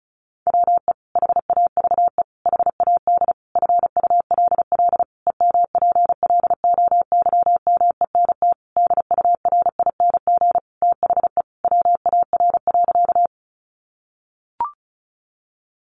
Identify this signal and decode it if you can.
Morse code, heard as a keyed tone, about 34 wpm, 700 Hz. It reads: WE HAVE HAD FULL EMPLOYMENT DURING THE WAR.
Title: Sentences from Top 1500 Words 35wpm